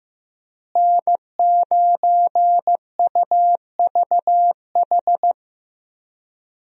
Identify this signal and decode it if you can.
Morse code, heard as a keyed tone, about 15 wpm, 700 Hz. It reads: N9UVH
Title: USA Call Signs - 5 Characters - Repeated 3x - 15wpm